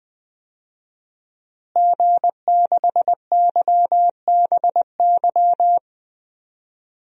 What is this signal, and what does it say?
Morse code, heard as a keyed tone, about 20 wpm, 700 Hz. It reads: G6YBY
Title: DXCC-100 Call Signs - 5 Characters - Repeated 2x - 20wpm